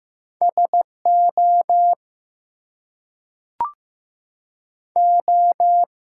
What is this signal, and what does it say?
Morse code, heard as a keyed tone, about 15 wpm, 700 Hz. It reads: SO O
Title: Two Letter Words - 15wpm